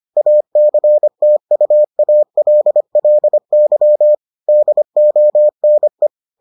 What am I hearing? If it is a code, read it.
Morse code, heard as a keyed tone, about 25 wpm, 600 Hz. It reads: ACTUALLY DONE